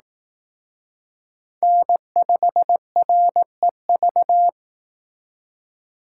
Morse code, heard as a keyed tone, about 19 wpm, 700 Hz. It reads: N5REV